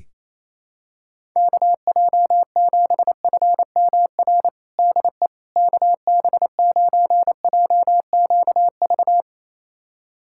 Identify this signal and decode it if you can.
Morse code, heard as a keyed tone, about 27 wpm, 700 Hz. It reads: KJ7FMR DE KB9JQV
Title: USA Callsign DE Callsign 28wpm